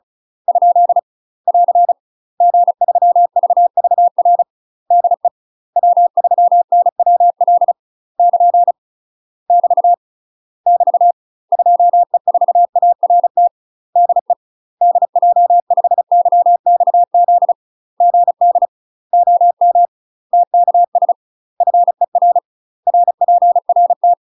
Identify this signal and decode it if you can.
Morse code, heard as a keyed tone, about 36 wpm, 700 Hz. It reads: ? <AR> G3VVR DE W3NWL <KN> <BT> <BT> 2E4ART DE DJ5YXZ GD OM TKS FER RPRT
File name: UK MASTER QSOs TRAINING / 35WPM / UK MASTER QSOs 0038